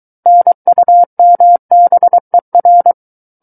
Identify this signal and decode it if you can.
Morse code, heard as a keyed tone, about 23 wpm, 700 Hz. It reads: NUMBER
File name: Book One / 14-Chapter 14 - 23 WPM